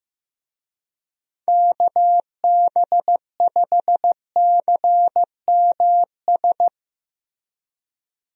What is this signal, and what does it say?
Morse code, heard as a keyed tone, about 15 wpm, 700 Hz. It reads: KB5CMS
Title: USA Call Signs - 6 Characters - Repeated 3x - 15wpm